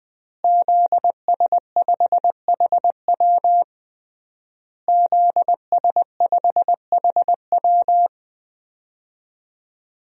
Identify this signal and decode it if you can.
Morse code, heard as a keyed tone, about 20 wpm, 700 Hz. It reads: ZS5HW ZS5HW
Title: DXCC-100 Call Signs - 5 Characters - Repeated 3x - 20wpm